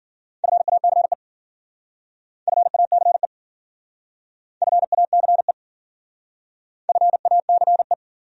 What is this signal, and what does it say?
Morse code, heard as a keyed tone, about 52 wpm, 700 Hz. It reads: FACE FACE FACE FACE